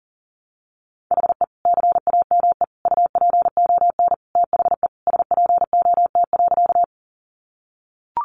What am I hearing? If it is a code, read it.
Morse code, heard as a keyed tone, about 40 wpm, 700 Hz. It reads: HE CAME UPON THE SPOT.
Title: Sentences from Top 1100 Words 40wpm